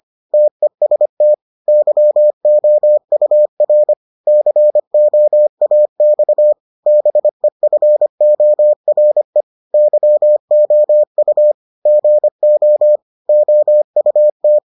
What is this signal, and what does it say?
Morse code, heard as a keyed tone, about 25 wpm, 600 Hz. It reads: TEST YOUR COAX BEFORE YOU GO OUT